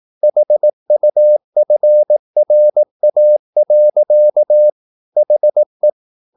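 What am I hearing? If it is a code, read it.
Morse code, heard as a keyed tone, about 18 wpm, 600 Hz. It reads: HUFRA. HE
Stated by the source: Morse practice at 18 wpm, English text, from CW med Gnister.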